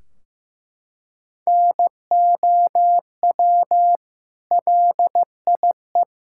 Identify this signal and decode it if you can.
Morse code, heard as a keyed tone, about 15 wpm, 700 Hz. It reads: NOW LIE
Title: Sentences from Top 1600 Words 15wpm